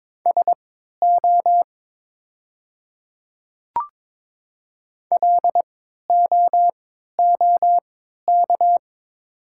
Morse code, heard as a keyed tone, about 23 wpm, spaced as for 13 wpm, 700 Hz. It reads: SO LOOK
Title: 3x Character Spacing - Top 100 Words - 22wpm